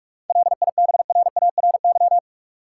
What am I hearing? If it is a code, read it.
Morse code, heard as a keyed tone, about 46 wpm, 700 Hz. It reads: LIBRARY
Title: Sentences from Top 600 Words - New Word Drill - 45wpm